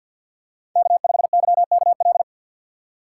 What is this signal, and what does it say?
Morse code, heard as a keyed tone, about 47 wpm, 700 Hz. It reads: K5YKL
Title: USA Call Signs - 5 Characters - 4x Character Spacing - 50wpm